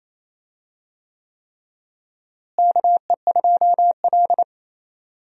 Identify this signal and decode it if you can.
Morse code, heard as a keyed tone, about 28 wpm, 700 Hz. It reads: KE2L
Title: USA Call Signs - 4 Characters - Repeated 3x - 28wpm